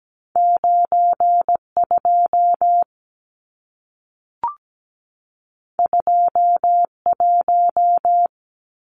Morse code, heard as a keyed tone, about 17 wpm, 700 Hz. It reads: 92 21
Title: Numbers - 2-Digit 17wpm